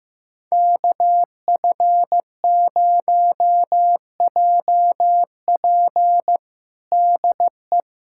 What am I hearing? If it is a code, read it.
Morse code, heard as a keyed tone, about 15 wpm, 700 Hz. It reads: KF0JP DE
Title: USA Callsign DE Callsign - 5 Characters - 15wpm